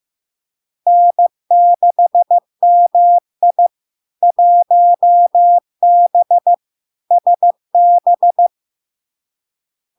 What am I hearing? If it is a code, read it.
Morse code, heard as a keyed tone, about 15 wpm, 700 Hz. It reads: N6MI 1B SB
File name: LARK-FD-2022-Log-in-CW-15wpm-700-Hz